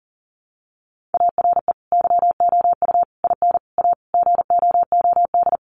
Morse code, heard as a keyed tone, about 40 wpm, 700 Hz. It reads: ARE YOU IN A GOOD
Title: Sentences from Top 1100 Words 40wpm